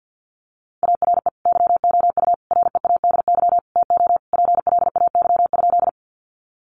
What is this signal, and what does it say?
Morse code, heard as a keyed tone, about 50 wpm, 700 Hz. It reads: ARE YOU READY TO PLAY?
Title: Sentences from Top 400 Words 50wpm